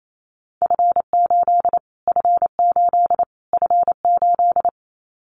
Morse code, heard as a keyed tone, about 28 wpm, 700 Hz. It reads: F8 F8 F8
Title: Callsign Prefixes for Top 25 DXCC - Repeated 3x - 28wpm